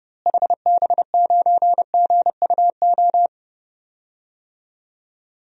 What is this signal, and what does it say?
Morse code, heard as a keyed tone, about 32 wpm, 700 Hz. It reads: HB9GUO
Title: DXCC-100 Call Signs - 6 Characters - Repeated 3x - 30wpm